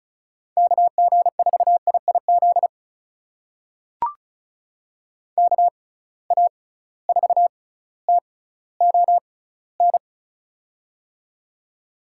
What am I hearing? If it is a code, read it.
Morse code, heard as a keyed tone, about 36 wpm, spaced as for 18 wpm, 700 Hz. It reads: KG4IIZ KA4TON
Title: USA Call Signs - 6 Characters - 6x Character Spacing - 35wpm